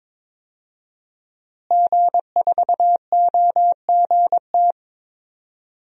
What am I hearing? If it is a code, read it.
Morse code, heard as a keyed tone, about 22 wpm, 700 Hz. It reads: G4OGT
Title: DXCC-100 Call Signs - 5 Characters - Repeated 2x - 22wpm